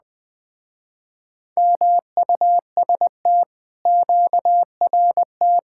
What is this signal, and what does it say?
Morse code, heard as a keyed tone, about 19 wpm, 700 Hz. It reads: MUST QRT
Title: Speed-Racing Common QSO Phrases - 20wpm